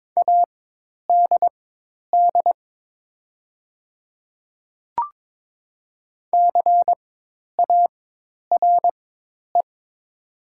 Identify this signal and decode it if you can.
Morse code, heard as a keyed tone, about 22 wpm, spaced as for 10 wpm, 700 Hz. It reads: ADD CARE
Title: 4x Character Spacing - Top 300 Words - 22wpm